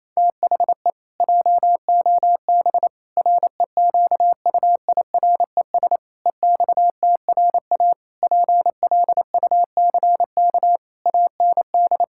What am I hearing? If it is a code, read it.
Morse code, heard as a keyed tone, about 28 wpm, 700 Hz. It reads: THE JOB REQUIRES EXTRA PLUCK AND
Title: Pangram Challenge v2 - 28wpm